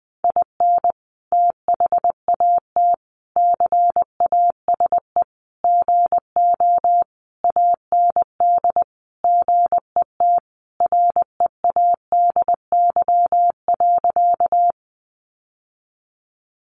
Morse code, heard as a keyed tone, about 20 wpm, 700 Hz. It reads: IN THAT CASE GO AND GET READY.
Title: Sentences from Top 700 Words 20wpm